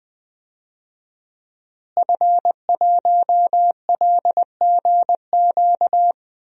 Morse code, heard as a keyed tone, about 20 wpm, 700 Hz. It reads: F1LGQ
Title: DXCC-100 Call Signs - 5 Characters - Repeated 2x - 20wpm